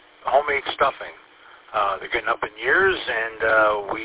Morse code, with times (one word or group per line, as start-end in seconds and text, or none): none